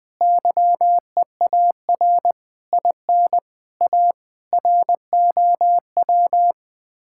Morse code, heard as a keyed tone, about 20 wpm, 700 Hz. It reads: YEAR IN A ROW
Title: Sets of 6 Words - 20wpm